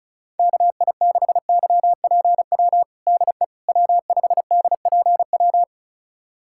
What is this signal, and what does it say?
Morse code, heard as a keyed tone, about 36 wpm, 700 Hz. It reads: KI6YPW DE W5DPW
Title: USA Callsign DE Callsign 35wpm